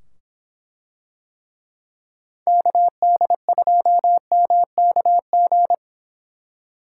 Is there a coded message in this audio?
Yes: Morse code from 2.5 to 5.7 s, heard as a keyed tone, about 24 wpm, 700 Hz. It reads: KD2MKG